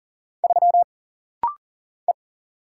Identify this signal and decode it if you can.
Morse code, heard as a keyed tone, about 42 wpm, 700 Hz. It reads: 3 E
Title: Single Letter-Number Alt - Rapid-Fire 40wpm